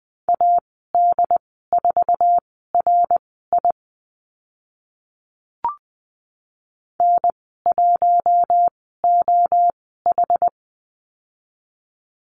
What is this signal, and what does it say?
Morse code, heard as a keyed tone, about 20 wpm, spaced as for 14 wpm, 700 Hz. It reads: AD4RI N1OH
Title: 082 Lesson 14g - Number 2 - Call Signs 20wpm